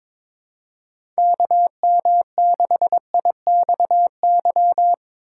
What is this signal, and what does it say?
Morse code, heard as a keyed tone, about 22 wpm, 700 Hz. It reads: KM6IXY
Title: USA Call Signs - 3x Character Spacing - 22wpm